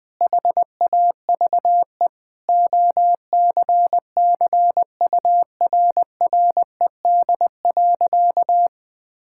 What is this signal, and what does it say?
Morse code, heard as a keyed tone, about 20 wpm, 700 Hz. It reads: HAVE OCCURRED.
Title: Sentences with 6-Word Phrases - Spoken Beginning - 20wpm